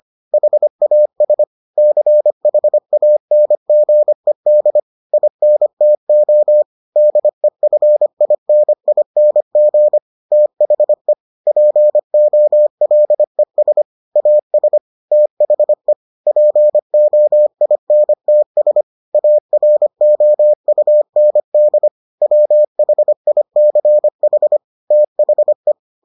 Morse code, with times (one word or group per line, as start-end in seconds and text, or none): 0.3-1.4 HAS
1.8-4.8 CHANGED
5.1-6.6 INTO
7.0-10.0 DEFINING
10.3-11.1 THE
11.5-13.8 POLES
14.2-14.8 AS
15.1-15.9 THE
16.3-18.8 POINTS
19.1-21.9 AROUND
22.2-24.6 WHICH
24.9-25.7 THE